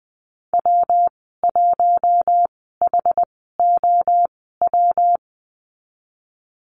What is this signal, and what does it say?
Morse code, heard as a keyed tone, about 20 wpm, spaced as for 15 wpm, 700 Hz. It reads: W1HOW
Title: 128 Lesson 21g - Number 6 - Call Signs 20wpm